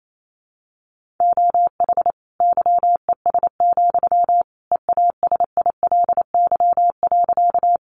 Morse code, heard as a keyed tone, about 28 wpm, 700 Hz. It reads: OH YES, EASILY.
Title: Sentences from Top 900 Words 28wpm